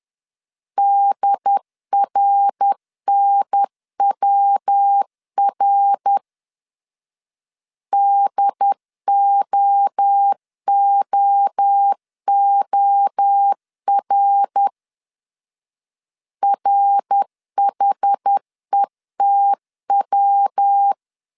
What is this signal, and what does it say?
Morse code, heard as a keyed tone, about 10 wpm, 800 Hz. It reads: DRNWR DOOOR RHETW
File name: CW / DET / 1 3 1500C S C 10wpm 3335